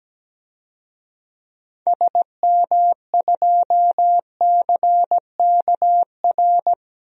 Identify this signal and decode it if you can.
Morse code, heard as a keyed tone, about 17 wpm, 700 Hz. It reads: SM2CKR
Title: DXCC-100 Call Signs - 6 Characters - 17wpm